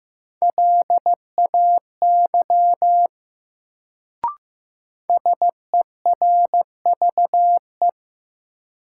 Morse code, heard as a keyed tone, about 16 wpm, 700 Hz. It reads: LAY SERVE
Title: Sentences from Top 400 Words - New Word Drill - 15wpm